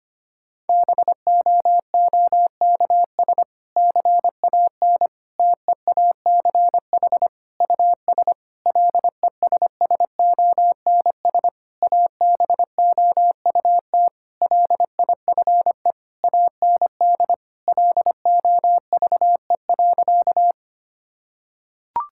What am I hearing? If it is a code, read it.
Morse code, heard as a keyed tone, about 25 wpm, 700 Hz. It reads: BOOKS CAN TEACH US LESSONS ABOUT LIFE AND LOVE.
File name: practice / 25wpm / Difficult Sentences with Hints - 25wpm